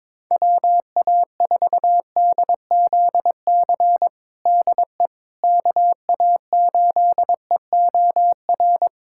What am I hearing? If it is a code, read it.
Morse code, heard as a keyed tone, about 22 wpm, 700 Hz. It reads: WA4DZC DE KA8EOR
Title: USA Callsign DE Callsign - 6 Characters - 22wpm